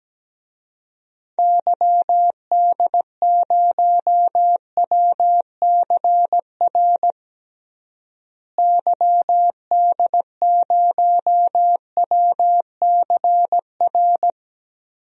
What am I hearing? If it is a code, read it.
Morse code, heard as a keyed tone, about 17 wpm, 700 Hz. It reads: YD0WCR YD0WCR
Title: DXCC-100 Call Signs - 6 Characters - Repeated 3x - 17wpm